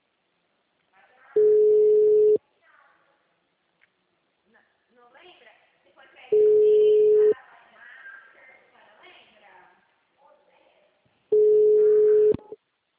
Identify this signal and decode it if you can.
Ringback tone (Europe, 425 Hz), with speech.
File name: out- -2015- - - .10010